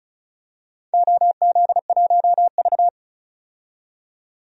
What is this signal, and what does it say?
Morse code, heard as a keyed tone, about 36 wpm, 700 Hz. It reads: OZ1V
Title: DXCC-100 Call Signs - 4 Characters - 35wpm